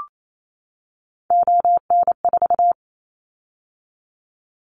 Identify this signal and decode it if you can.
Morse code, heard as a keyed tone, about 28 wpm, 700 Hz. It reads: ON4
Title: Callsign Prefixes for Top 50 DXCC - 28wpm